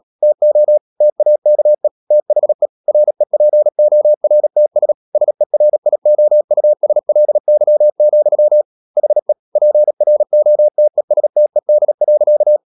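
Morse code, heard as a keyed tone, about 37 wpm, 600 Hz. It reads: TO TAKE THE REPORTS SERIOUSLY, HE PROTESTED.
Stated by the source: Morse code practice at 35–39 wpm sending English text of Ken Follett's novel Never, from CW med Gnister.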